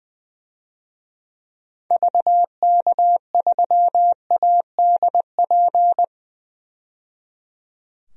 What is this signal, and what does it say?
Morse code, heard as a keyed tone, about 20 wpm, 700 Hz. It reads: VK3ADP
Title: DXCC-100 Call Signs - 6 Characters - 20wpm